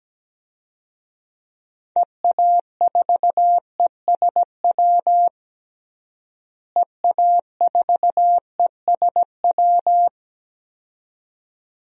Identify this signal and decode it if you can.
Morse code, heard as a keyed tone, about 17 wpm, 700 Hz. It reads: EA4ESW EA4ESW
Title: DXCC-100 Call Signs - 6 Characters - Repeated 2x - 17wpm